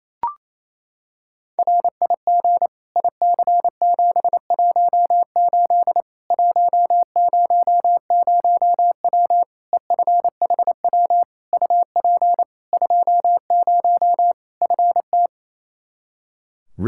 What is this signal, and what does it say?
Morse code, heard as a keyed tone, about 27 wpm, 700 Hz. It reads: RIG IC718 100W EFHW UP 20 FT
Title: Practice QSO - Exchange 2 - Info - 28wpm